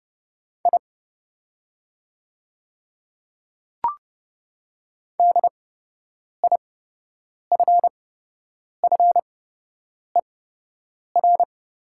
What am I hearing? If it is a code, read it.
Morse code, heard as a keyed tone, about 30 wpm, spaced as for 8 wpm, 700 Hz. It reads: I DIFFER
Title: 8x Character Spacing - Top 300 Words - 30wpm